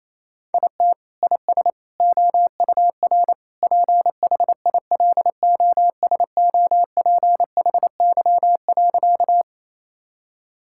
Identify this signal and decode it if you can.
Morse code, heard as a keyed tone, about 28 wpm, 700 Hz. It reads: IT IS OUR PHILOSOPHY.
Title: Sentences from Top 1000 Words - Review Part 6 - 28wpm